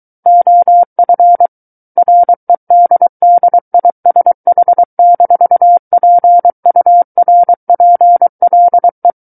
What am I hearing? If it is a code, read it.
Morse code, heard as a keyed tone, about 23 wpm, 700 Hz. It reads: OF REDDISH-PURPLE
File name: Book One / 14-Chapter 14 - 23 WPM